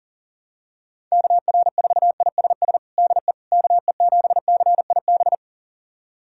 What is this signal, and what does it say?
Morse code, heard as a keyed tone, about 40 wpm, 700 Hz. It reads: KR4ISS DE KE7CIB
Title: USA Callsign DE Callsign - 6 Characters - 40wpm